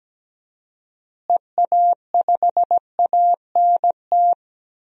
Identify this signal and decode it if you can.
Morse code, heard as a keyed tone, about 17 wpm, 700 Hz. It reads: EA5ANT
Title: DXCC-100 Call Signs - 6 Characters - Repeated 2x - 17wpm